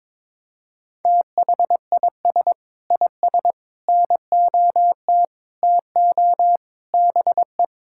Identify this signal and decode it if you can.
Morse code, heard as a keyed tone, about 22 wpm, 700 Hz. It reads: THIS IS NOT TO BE
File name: Sentences with 7-Word Phrases - 22wpm